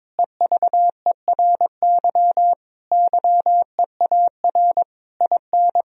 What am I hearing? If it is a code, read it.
Morse code, heard as a keyed tone, about 22 wpm, 700 Hz. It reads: EVERY YEAR IN
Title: Sentences from Top 1000 Words - Review Part 1 - 22wpm